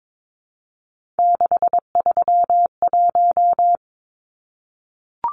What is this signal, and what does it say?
Morse code, heard as a keyed tone, about 22 wpm, 700 Hz. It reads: 631
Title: Numbers - 3-Digit 22wpm